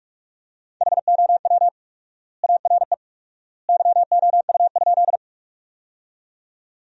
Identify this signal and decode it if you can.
Morse code, heard as a keyed tone, about 47 wpm, 700 Hz. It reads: HOW ARE YOU?
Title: 4x Word-Spacing - Sentences from Top 100 Words - 45wpm